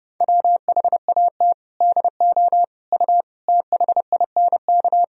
Morse code, heard as a keyed tone, about 30 wpm, 700 Hz. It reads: WHAT DO U THINK
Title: Sentences from Top 200 Words and Abbreviations 30wpm